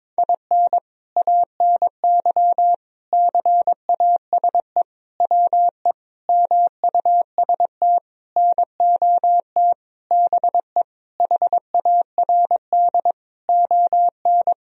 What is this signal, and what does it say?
Morse code, heard as a keyed tone, about 22 wpm, 700 Hz. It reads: IN ANY CASE WE MUST NOT BE HARD ON
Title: Sentences from Top 1000 Words - Review Part 1 - 22wpm